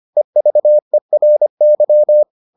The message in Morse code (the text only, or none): EVERY